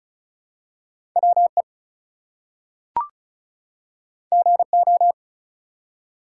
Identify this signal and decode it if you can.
Morse code, heard as a keyed tone, about 33 wpm, 700 Hz. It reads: WE GO
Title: Two Letter Words - 35wpm